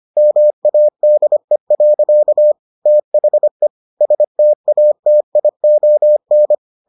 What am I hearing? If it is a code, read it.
Morse code, heard as a keyed tone, about 25 wpm, 600 Hz. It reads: MADE. THE STATION